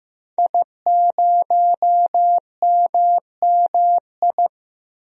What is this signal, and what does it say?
Morse code, heard as a keyed tone, about 15 wpm, 700 Hz. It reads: I0MMI
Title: DXCC-100 Call Signs - 5 Characters - Repeated 3x - 15wpm